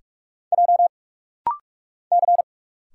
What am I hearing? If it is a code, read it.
Morse code, heard as a keyed tone, about 44 wpm, 700 Hz. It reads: J C
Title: Single Letters - Rapid-Fire 45wpm